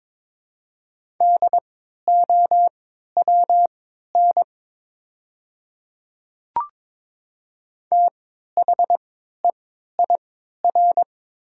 Morse code, heard as a keyed tone, about 22 wpm, spaced as for 13 wpm, 700 Hz. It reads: DOWN THEIR
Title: 3x Character Spacing - Top 300 Words - 22wpm